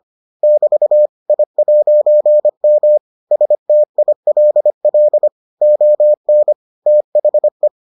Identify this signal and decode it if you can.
Morse code, heard as a keyed tone, about 25 wpm, 600 Hz. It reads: <BT> I'M STILL ON THE